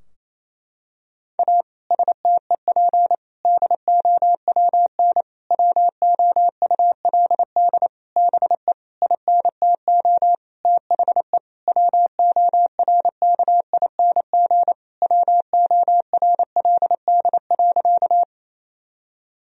Morse code, heard as a keyed tone, about 28 wpm, 700 Hz. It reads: A STEP DOWN WOULD BE INTO THE WORKING WORLD.
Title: Sentences from Top 1000 Words - Review Part 4 - 28wpm